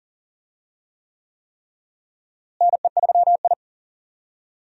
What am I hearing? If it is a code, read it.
Morse code, heard as a keyed tone, about 40 wpm, 700 Hz. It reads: NE3I
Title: USA Call Signs - 4 Characters - Repeated 3x - 40wpm